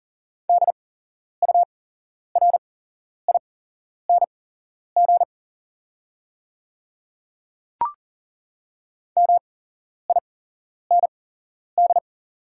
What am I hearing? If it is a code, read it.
Morse code, heard as a keyed tone, about 37 wpm, spaced as for 11 wpm, 690 Hz. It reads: DURING MIND
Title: 8x Character Spacing - Top 500 Words - 40wpm